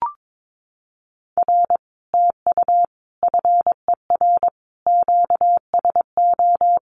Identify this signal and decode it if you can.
Morse code, heard as a keyed tone, about 22 wpm, 700 Hz. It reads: R TU FER QSO
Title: Common QSO Phrases Encore 22wpm